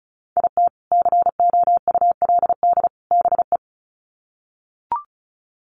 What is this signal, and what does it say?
Morse code, heard as a keyed tone, about 34 wpm, 700 Hz. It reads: IT COULD BE
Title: Sets of 3 Words 35wpm